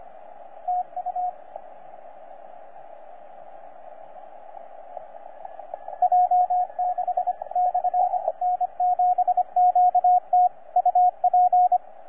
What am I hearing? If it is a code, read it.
Morse code, heard as a keyed tone, about 25 wpm, 720 Hz, with radio noise. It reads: TU J6/N7QT UP
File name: J6/N7QT 20CW